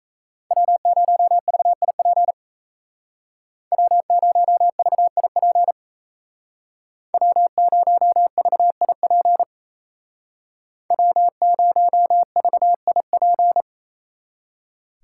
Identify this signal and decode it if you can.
Morse code, heard as a keyed tone, about 33 wpm, 700 Hz. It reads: W0VIP W0VIP W0VIP W0VIP
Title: Speed Racing - USA 1x3 Call Signs 28wpm